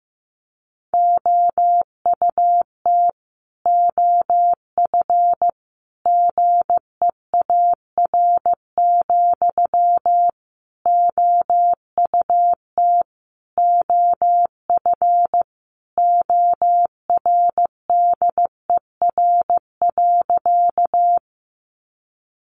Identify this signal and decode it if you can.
Morse code, heard as a keyed tone, about 15 wpm, 700 Hz. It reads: OUT OF GEAR, OUT OF ORDER.
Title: Sentences from Top 1400 Words 15wpm